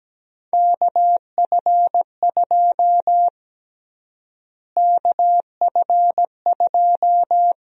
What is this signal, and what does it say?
Morse code, heard as a keyed tone, about 17 wpm, 700 Hz. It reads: KF2 KF2
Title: USA Call Sign Prefixes - Repeated 2x - 17wpm